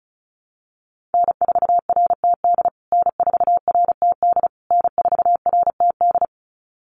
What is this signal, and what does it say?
Morse code, heard as a keyed tone, about 34 wpm, 700 Hz. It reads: N4RTD N4RTD N4RTD
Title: USA Call Signs Repeated 3x 35wpm